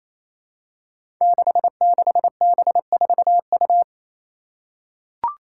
Morse code, heard as a keyed tone, about 28 wpm, 700 Hz. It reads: 66B4U
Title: Five Challenging Letter-Number Combinations 28wpm